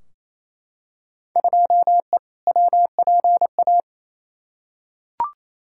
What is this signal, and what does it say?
Morse code, heard as a keyed tone, about 28 wpm, 700 Hz. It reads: 2E WPA
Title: Field Day Exchanges Encore - 28wpm